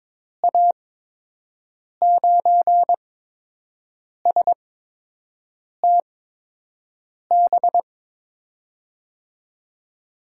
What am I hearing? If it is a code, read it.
Morse code, heard as a keyed tone, about 22 wpm, spaced as for 6 wpm, 700 Hz. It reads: A9STB
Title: USA Call Signs - 6 Characters - 8x Character Spacing - 22wpm